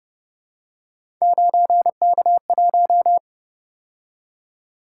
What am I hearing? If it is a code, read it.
Morse code, heard as a keyed tone, about 30 wpm, 700 Hz. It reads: 9K1